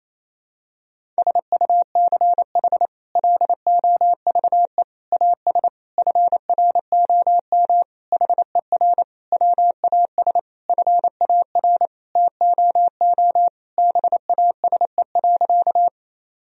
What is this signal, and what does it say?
Morse code, heard as a keyed tone, about 28 wpm, 700 Hz. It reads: SUCH LOVE AS FROM HER WAS FAR TOO BASE.
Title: Sentences from Top 1000 Words - Review Part 1 - 28wpm